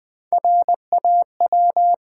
Morse code, heard as a keyed tone, about 20 wpm, 700 Hz. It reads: RAW
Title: Three Letter Words - 20wpm